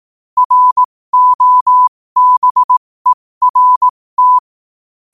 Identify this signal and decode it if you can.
Morse code, heard as a keyed tone, about 18 wpm, spaced as for 15 wpm, 1000 Hz. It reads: ROBERT
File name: qotdfiles / 15wpm